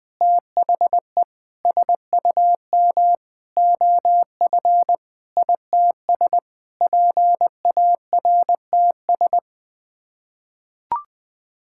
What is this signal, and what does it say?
Morse code, heard as a keyed tone, about 20 wpm, 700 Hz. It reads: THE SUM OF ITS PARTS
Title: Sets of 8 Words - 20wpm